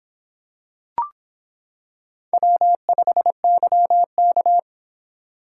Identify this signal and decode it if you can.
Morse code, heard as a keyed tone, about 26 wpm, 700 Hz. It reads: W5YK